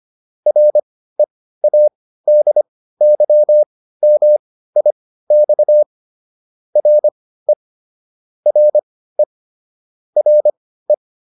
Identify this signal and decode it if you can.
Morse code, heard as a keyed tone, about 25 wpm, spaced as for 15 wpm, 600 Hz. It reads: READYMIX RE RE RE